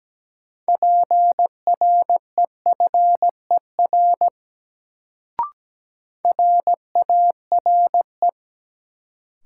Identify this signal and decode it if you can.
Morse code, heard as a keyed tone, about 17 wpm, 700 Hz. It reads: PREFER RARE
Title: Sentences from Top 1100 Words - New Word Drill - 17wpm